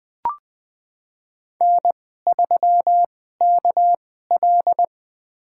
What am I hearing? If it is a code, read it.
Morse code, heard as a keyed tone, about 20 wpm, spaced as for 15 wpm, 700 Hz. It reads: N3KL
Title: USA Call Signs - 4 Characters - 2x Character Spacing - 20wpm